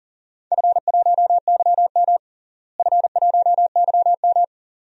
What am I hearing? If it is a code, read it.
Morse code, heard as a keyed tone, about 40 wpm, 700 Hz. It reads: F1YM F1YM
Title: DXCC-100 Call Signs - 4 Characters - Repeated 2x - 40wpm